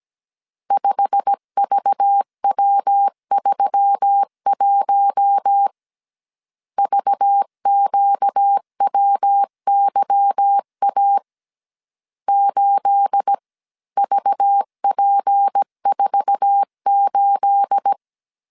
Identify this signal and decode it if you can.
Morse code, heard as a keyed tone, about 17 wpm, 800 Hz. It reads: 5VW31 VQWYA 8VP48
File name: CW / DET / 1 9 1500C S C 16wpm 2909